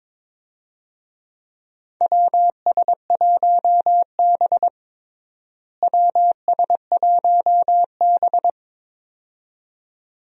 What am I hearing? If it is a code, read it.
Morse code, heard as a keyed tone, about 22 wpm, 700 Hz. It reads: WS1B WS1B
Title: USA Call Signs - 4 Characters - Repeated 2x - 22wpm